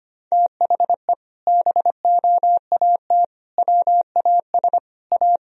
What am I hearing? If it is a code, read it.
Morse code, heard as a keyed tone, about 25 wpm, 700 Hz. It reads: THE BOAT WAS A
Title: Sentences from Top 1000 Words - Review Part 3 - 25wpm